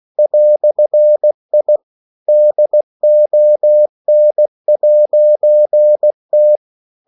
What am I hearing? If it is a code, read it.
Morse code, heard as a keyed tone, about 16 wpm, 600 Hz. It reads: "I DON'T
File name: CW / Practice / EnglishQuotations0416